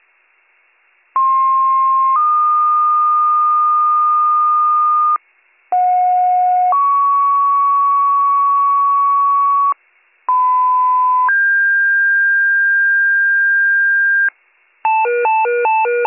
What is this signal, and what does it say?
Two-tone page, heard as a one-second tone, then a three-second tone, about 1060 then 1220 Hz, about 730 then 1120 Hz, about 1010 then 1600 Hz. It is heard from 1.2 to 14.3 s.